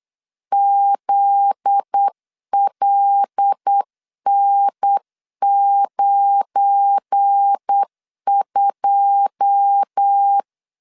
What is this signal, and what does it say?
Morse code, heard as a keyed tone, about 8 wpm, 800 Hz. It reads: ZLN92
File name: LEZIONI / 9 28 8wpm 3512